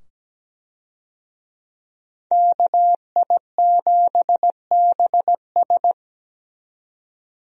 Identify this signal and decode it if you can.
Morse code, heard as a keyed tone, about 17 wpm, 700 Hz. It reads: KI7BS
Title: USA Call Signs - 5 Characters - 17wpm